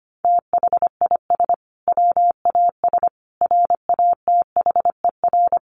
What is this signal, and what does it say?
Morse code, heard as a keyed tone, about 25 wpm, 700 Hz. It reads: THIS WAS RATHER
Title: Sentences from Top 1400 Words 25wpm